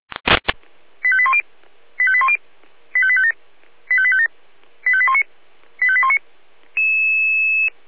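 Five-tone selective call (ZVEI), heard as quick runs of five tones, tones of about 70 ms. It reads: 85810 85810 85857 85857 85819 85819
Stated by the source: Five-tone ZVEI selective call received by SDR, logged on the Feuerwehr Orlamünde (fire brigade) audio server.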